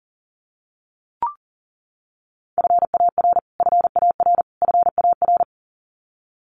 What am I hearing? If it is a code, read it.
Morse code, heard as a keyed tone, about 40 wpm, 700 Hz. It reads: FAR FAR FAR
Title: Top 300 Words Repeated 40wpm